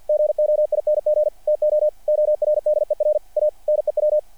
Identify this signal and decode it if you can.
Morse code, heard as a keyed tone, about 48 wpm, 600 Hz. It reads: GOING TO ORDER A NEW
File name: archived / CRSnet